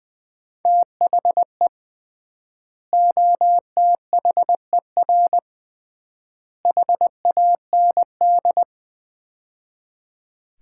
Morse code, heard as a keyed tone, about 20 wpm, 700 Hz. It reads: THE OTHER HAND
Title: 3x Word-Spacing - Sets of 3 Words - 20wpm